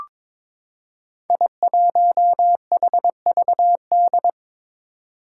Morse code, heard as a keyed tone, about 23 wpm, 700 Hz. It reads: I1HVD
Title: Five Challenging Letter-Number Combinations 22wpm